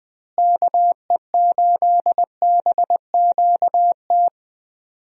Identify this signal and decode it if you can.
Morse code, heard as a keyed tone, about 20 wpm, 700 Hz. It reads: KE8BQT
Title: USA Call Signs - 6 Characters - Repeated 2x - 20wpm